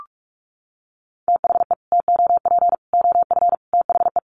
Morse code, heard as a keyed tone, about 44 wpm, 700 Hz. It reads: THE TOP OF THE
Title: Sets of 5 Words Encore2 45wpm